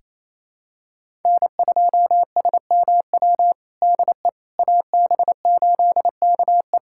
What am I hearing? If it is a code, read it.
Morse code, heard as a keyed tone, about 28 wpm, 700 Hz. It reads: N2SMW DE AB8KE
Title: USA Callsign DE Callsign - 5 Characters - 28wpm